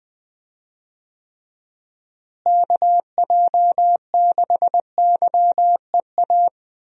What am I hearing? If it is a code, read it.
Morse code, heard as a keyed tone, about 20 wpm, 700 Hz. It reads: KJ6YEA
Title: USA Call Signs - 6 Characters - Repeated 2x - 20wpm